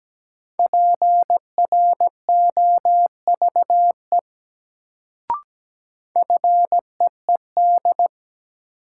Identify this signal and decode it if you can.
Morse code, heard as a keyed tone, about 17 wpm, 700 Hz. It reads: PROVE FEED
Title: Sentences from Top 1600 Words - New Word Drill - 17wpm